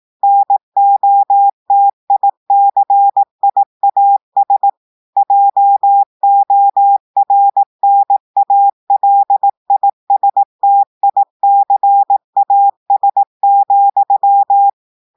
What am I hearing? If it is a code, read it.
Morse code, heard as a keyed tone, about 18 wpm, 800 Hz. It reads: NOTICIAS JORNALISTICAS,
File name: audios / portugais / portugais texte en clair 2 vitesse 18 mots minutes